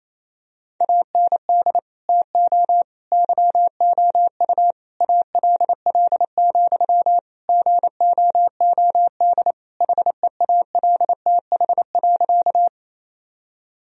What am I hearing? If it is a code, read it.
Morse code, heard as a keyed tone, about 27 wpm, 700 Hz. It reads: AND TO YOU ALL, GOOD HEALTH.
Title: Sentences from Top 1000 Words - Review Part 3 - 28wpm